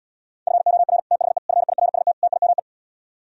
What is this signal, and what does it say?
Morse code, heard as a keyed tone, about 77 wpm, 700 Hz. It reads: FOR THE FIRST TIME